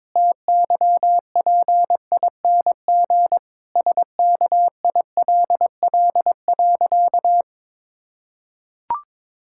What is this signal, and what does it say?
Morse code, heard as a keyed tone, about 22 wpm, 700 Hz. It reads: TYPING SKILL.
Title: Pangram Challenge v2 - 22wpm